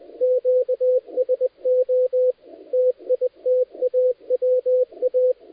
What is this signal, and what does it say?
Morse code, heard as a keyed tone, about 20 wpm, 510 Hz, with radio noise. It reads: QSO TITAWA
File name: dlazaawansowanych / nagraniazpasma / titawaNETcontrol